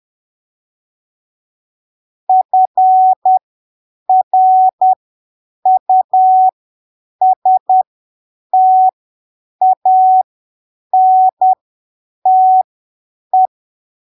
Morse code, heard as a keyed tone, about 10 wpm, spaced as for 7 wpm, 750 Hz. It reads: FRUSTANTE
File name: arquivos / texto / texto 23 10ppm